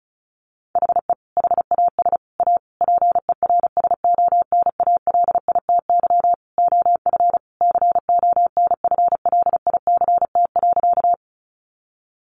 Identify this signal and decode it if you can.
Morse code, heard as a keyed tone, about 34 wpm, 700 Hz. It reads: HE HAS A PERSONALITY OF CONFLICT.
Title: Sentences from Top 1400 Words 35wpm